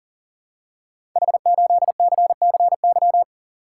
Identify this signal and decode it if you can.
Morse code, heard as a keyed tone, about 40 wpm, 700 Hz. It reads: H8CCY
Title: Code Groups - 5 Letter:Numbers - 40wpm